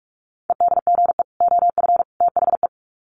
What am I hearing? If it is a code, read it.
Morse code, heard as a keyed tone, about 44 wpm, 700 Hz. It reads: EDGE OF THE
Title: Sets of 3 Words 45wpm